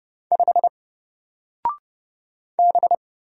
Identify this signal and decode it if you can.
Morse code, heard as a keyed tone, about 30 wpm, 700 Hz. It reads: 5 B
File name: Single-Letter-Number-30wpm